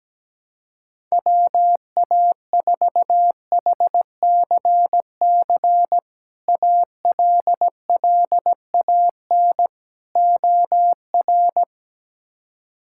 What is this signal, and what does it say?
Morse code, heard as a keyed tone, about 17 wpm, 700 Hz. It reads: WA4HCC ALLAN OR
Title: K1USN SST QSO Practice - 17wpm